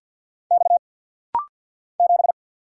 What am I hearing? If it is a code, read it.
Morse code, heard as a keyed tone, about 50 wpm, 690 Hz. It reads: X 7
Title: Single Letter-Number Alt - Rapid-Fire 50wpm